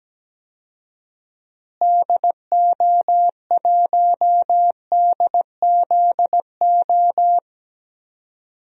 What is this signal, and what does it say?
Morse code, heard as a keyed tone, about 17 wpm, 700 Hz. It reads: DO1DZO
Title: DXCC-100 Call Signs - 6 Characters - Repeated 2x - 17wpm